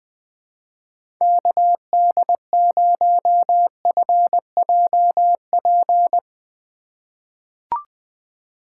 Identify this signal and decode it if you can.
Morse code, heard as a keyed tone, about 21 wpm, 700 Hz. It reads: KD0FJP
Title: USA Call Signs - 6 Characters - 3x Character Spacing - 20wpm